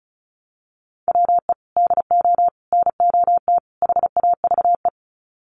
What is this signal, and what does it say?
Morse code, heard as a keyed tone, about 34 wpm, 700 Hz. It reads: WE DO NOT HAVE
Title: Sets of 4 Words 35wpm